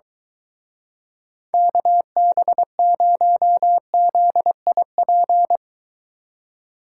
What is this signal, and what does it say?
Morse code, heard as a keyed tone, about 23 wpm, 700 Hz. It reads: KB0ZIP